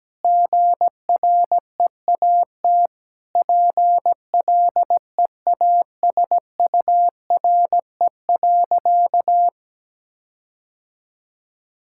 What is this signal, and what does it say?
Morse code, heard as a keyed tone, about 17 wpm, 700 Hz. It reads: GREAT PLEASURE.
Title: Sentences from Top 1600 Words 17wpm